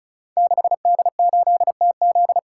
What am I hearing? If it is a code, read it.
Morse code, heard as a keyed tone, about 36 wpm, 700 Hz. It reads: 6D8TZ
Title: Code Groups - 5 Letter:Numbers - 35wpm